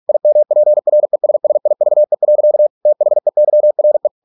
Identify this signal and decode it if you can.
Morse code, heard as a keyed tone, about 46 wpm, 600 Hz. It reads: IMPRESSIVE. THEYRE